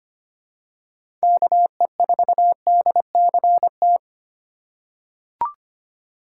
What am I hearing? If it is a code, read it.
Morse code, heard as a keyed tone, about 25 wpm, 700 Hz. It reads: KE4DCT
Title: USA Call Signs - 6 Characters - 4x Character Spacing - 25wpm